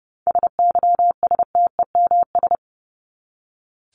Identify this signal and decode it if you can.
Morse code, heard as a keyed tone, about 30 wpm, 700 Hz. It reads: SYSTEMS
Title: ES:S Suffix - Top 100 Words with Base Words 30wpm